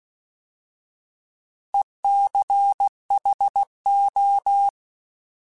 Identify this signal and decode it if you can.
Morse code, heard as a keyed tone, about 16 wpm, 800 Hz. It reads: ECHO
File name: Alphabetical / Alphabetical mp3s / Es @15